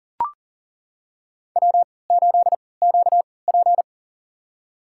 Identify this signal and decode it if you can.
Morse code, heard as a keyed tone, about 40 wpm, spaced as for 23 wpm, 700 Hz. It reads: W8QP
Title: USA Call Signs - 4 Characters - 3x Character Spacing - 40wpm